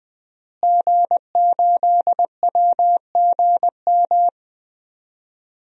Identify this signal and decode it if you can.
Morse code, heard as a keyed tone, about 20 wpm, 700 Hz. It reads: G8WGM
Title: DXCC-100 Call Signs - 5 Characters - Repeated 3x - 20wpm